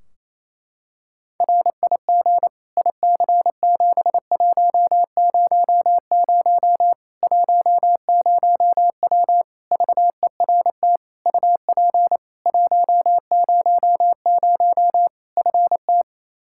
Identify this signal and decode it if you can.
Morse code, heard as a keyed tone, about 28 wpm, 700 Hz. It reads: RIG IC7100 10W VERT UP 100 FT
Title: Practice QSO - Exchange 2 - Info - 28wpm